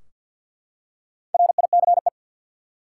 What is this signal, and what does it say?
Morse code, heard as a keyed tone, about 52 wpm, 700 Hz. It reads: RICE
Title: Four Letter Words - 50wpm